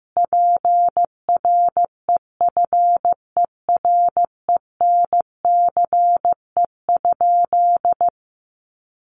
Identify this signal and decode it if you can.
Morse code, heard as a keyed tone, about 15 wpm, 700 Hz. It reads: PREFERENCE?
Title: Sentences from Top 1500 Words 15wpm